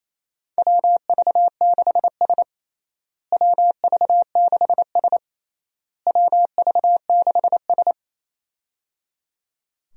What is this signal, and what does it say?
Morse code, heard as a keyed tone, about 28 wpm, 700 Hz. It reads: WV6S WV6S WV6S
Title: USA Call Signs - 4 Characters - Repeated 3x - 28wpm